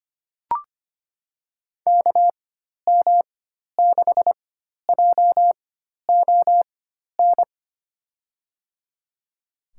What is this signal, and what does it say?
Morse code, heard as a keyed tone, about 25 wpm, spaced as for 12 wpm, 700 Hz. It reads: KM6JON
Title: USA Call Signs - 6 Characters - 4x Character Spacing - 25wpm